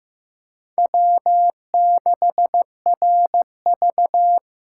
Morse code, heard as a keyed tone, about 15 wpm, 700 Hz. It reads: W6RV
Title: USA Call Signs - 4 Characters - 4x Character Spacing - 15wpm